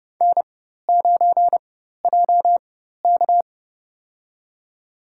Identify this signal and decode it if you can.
Morse code, heard as a keyed tone, about 31 wpm, spaced as for 14 wpm, 700 Hz. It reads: N9JK
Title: USA Call Signs - 4 Characters - 4x Character Spacing - 30wpm